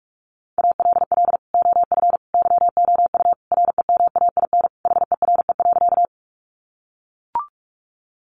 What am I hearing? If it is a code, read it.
Morse code, heard as a keyed tone, about 44 wpm, 700 Hz. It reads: ALL OF YOU REMAIN HERE.
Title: Sentences from Top 1200 Words 45wpm